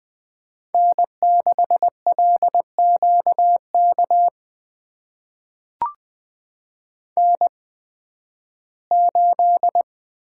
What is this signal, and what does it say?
Morse code, heard as a keyed tone, about 20 wpm, spaced as for 13 wpm, 700 Hz. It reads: N6LQK N8
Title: USA Call Signs - 8x Character Spacing - 20wpm